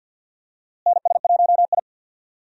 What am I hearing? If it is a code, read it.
Morse code, heard as a keyed tone, about 47 wpm, 700 Hz. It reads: NS1I
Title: USA Call Signs - 4 Characters - Repeated 3x - 50wpm